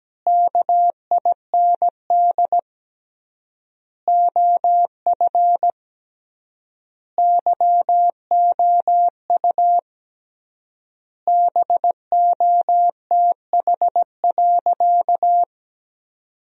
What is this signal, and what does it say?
Morse code, heard as a keyed tone, about 17 wpm, 700 Hz. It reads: KIND OF YOU BOTH.
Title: 3x Word-Spacing - Sentences from Top 300 Words - 17wpm